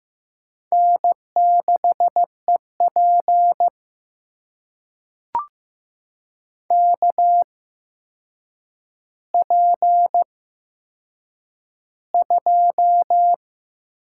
Morse code, heard as a keyed tone, about 15 wpm, spaced as for 7 wpm, 700 Hz. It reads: N6EP KP2
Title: USA Call Signs - 4 Characters - 8x Character Spacing - 15wpm